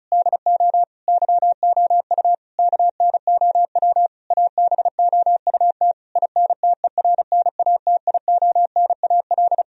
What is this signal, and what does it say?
Morse code, heard as a keyed tone, about 36 wpm, 700 Hz. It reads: DO YOU KNOW ABOUT INTERNATIONAL
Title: Sentences from Top 1000 Words - Review Part 4 - 35wpm